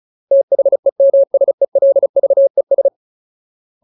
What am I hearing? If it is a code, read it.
Morse code, heard as a keyed tone, about 34 wpm, 550 Hz. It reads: THEMSELVES
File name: Ultimate Word List / 35WPM Ultimate